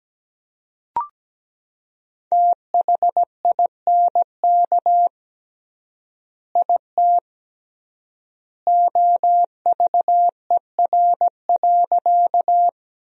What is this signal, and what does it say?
Morse code, heard as a keyed tone, about 17 wpm, 700 Hz. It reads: THINK IT OVER.
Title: 3x Word-Spacing - Sentences from Top 200 Words - 17wpm